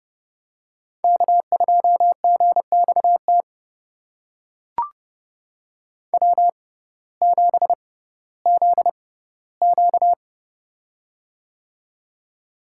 Morse code, heard as a keyed tone, about 31 wpm, spaced as for 17 wpm, 700 Hz. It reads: K2GXT W7ZQ
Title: USA Call Signs - 6x Character Spacing - 30wpm